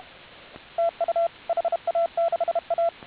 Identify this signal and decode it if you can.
Morse code, heard as a keyed tone, about 32 wpm, 1350 Hz, with radio noise. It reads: TU HA6A